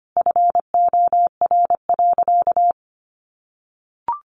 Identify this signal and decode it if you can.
Morse code, heard as a keyed tone, about 25 wpm, 700 Hz. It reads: FOR.
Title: A Sound Of Thunder 25wpm